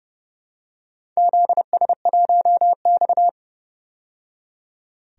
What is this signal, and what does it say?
Morse code, heard as a keyed tone, about 30 wpm, 700 Hz. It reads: ZS1X
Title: DXCC-100 Call Signs - 4 Characters - 30wpm